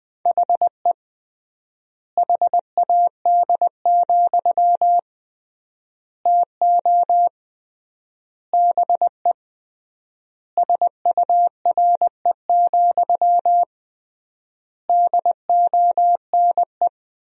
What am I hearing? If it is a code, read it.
Morse code, heard as a keyed tone, about 20 wpm, 700 Hz. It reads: HE HAD, TO BE SURE, DONE
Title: 3x Word-Spacing - Sentences from Top 500 Words - 20wpm